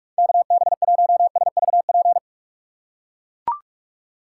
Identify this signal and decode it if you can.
Morse code, heard as a keyed tone, about 44 wpm, 700 Hz. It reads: KB1SVP
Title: USA Call Signs - 6 Characters - 6x Character Spacing - 45wpm